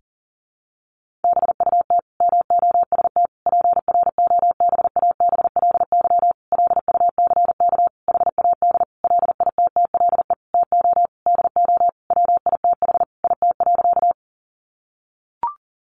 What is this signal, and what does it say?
Morse code, heard as a keyed tone, about 40 wpm, 700 Hz. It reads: BUT MOST PROBABLY LUCK HAD LITTLE TO DO WITH IT.
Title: Sentences from Top 1500 Words 40wpm